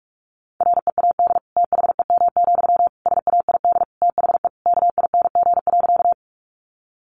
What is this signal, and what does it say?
Morse code, heard as a keyed tone, about 44 wpm, 700 Hz. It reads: READ THEM, SAID THE KING.
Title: Sentences from Top 400 Words 45wpm